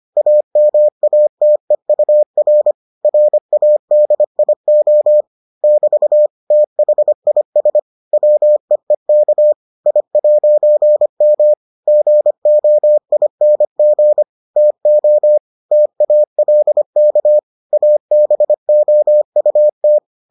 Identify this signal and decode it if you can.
Morse code, heard as a keyed tone, about 25 wpm, 600 Hz. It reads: AMATEUR RADIO <BT> THIS WEEK I'M GOING TO TALK ABOUT